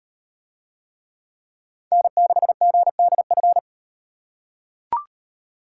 Morse code, heard as a keyed tone, about 37 wpm, 700 Hz. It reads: N6GDF